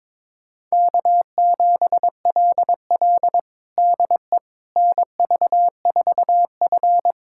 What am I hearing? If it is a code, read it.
Morse code, heard as a keyed tone, about 22 wpm, 700 Hz. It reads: K7LL DE NV4F
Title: USA Callsign DE Callsign - 4 Characters - 22wpm